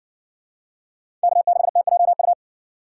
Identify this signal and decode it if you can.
Morse code, heard as a keyed tone, about 60 wpm, 700 Hz. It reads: K6TJU